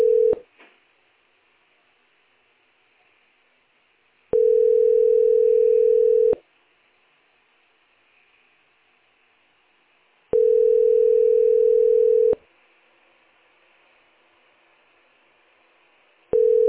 Ringback tone (North America), 440 + 480 Hz.